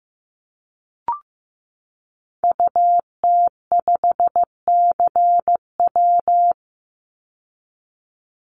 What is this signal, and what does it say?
Morse code, heard as a keyed tone, about 15 wpm, 700 Hz. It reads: UT5CW
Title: Call Signs Repeated 2x - 15wpm